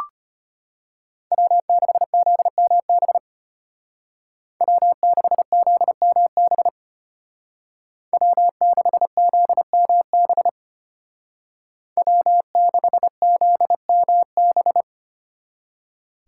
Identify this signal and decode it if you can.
Morse code, heard as a keyed tone, about 32 wpm, 700 Hz. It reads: W6ZMB W6ZMB W6ZMB W6ZMB